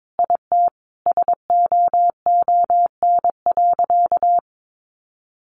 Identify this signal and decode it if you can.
Morse code, heard as a keyed tone, about 22 wpm, 700 Hz. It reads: IT SOON.
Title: Sentences from Top 1100 Words 22wpm